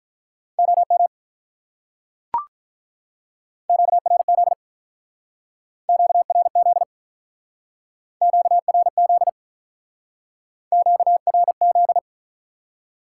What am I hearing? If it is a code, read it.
Morse code, heard as a keyed tone, about 45 wpm, 700 Hz. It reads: OM QRZ QRZ QRZ QRZ